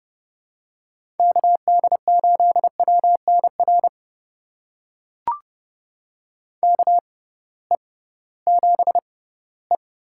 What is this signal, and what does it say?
Morse code, heard as a keyed tone, about 29 wpm, spaced as for 17 wpm, 700 Hz. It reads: KD8WNR KE7E
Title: USA Call Signs - 6 Characters - 6x Character Spacing - 30wpm